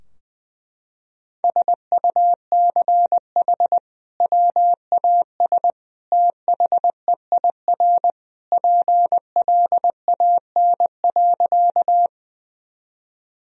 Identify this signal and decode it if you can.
Morse code, heard as a keyed tone, about 21 wpm, 700 Hz. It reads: SUCH WAS THEIR PLAN.
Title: Sentences from Top 1000 Words - Review Part 2 - 20wpm